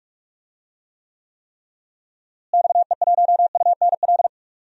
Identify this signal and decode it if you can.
Morse code, heard as a keyed tone, about 46 wpm, 700 Hz. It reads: XE1UNL
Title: DXCC-100 Call Signs - 6 Characters - Repeated 3x - 45wpm